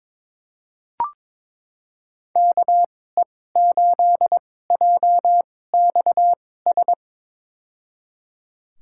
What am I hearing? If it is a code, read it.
Morse code, heard as a keyed tone, about 22 wpm, spaced as for 16 wpm, 700 Hz. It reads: KE8JXS
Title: USA Call Signs - 2x Character Spacing - 22wpm